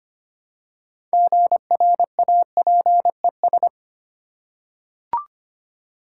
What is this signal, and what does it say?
Morse code, heard as a keyed tone, about 25 wpm, 690 Hz. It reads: GRAPES
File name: practice / 25wpm / Easy Words with Hints - 25wpm